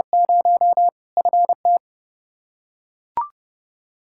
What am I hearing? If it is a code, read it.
Morse code, heard as a keyed tone, about 30 wpm, 700 Hz. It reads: 0 FT
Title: Practice QSO - Exchange 2 - Info - 30wpm